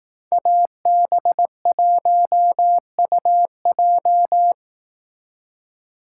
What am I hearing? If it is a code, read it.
Morse code, heard as a keyed tone, about 18 wpm, 700 Hz. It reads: AB1UJ
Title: Speed Racing - USA Call Signs 15wpm